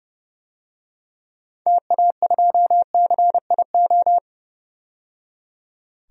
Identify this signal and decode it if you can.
Morse code, heard as a keyed tone, about 30 wpm, 700 Hz. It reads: TA2CIO
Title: DXCC-100 Call Signs - 6 Characters - 30wpm